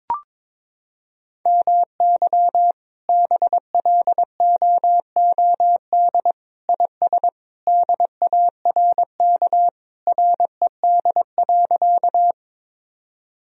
Morse code, heard as a keyed tone, about 21 wpm, 700 Hz. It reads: MY BLOOD IS DARK RED.
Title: Sentences from Top 1000 Words - Review Part 2 - 22wpm